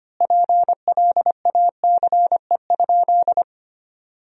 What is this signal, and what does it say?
Morse code, heard as a keyed tone, about 25 wpm, 700 Hz. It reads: PLACE?
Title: 4x Word-Spacing - Sentences from Top 200 Words - 25wpm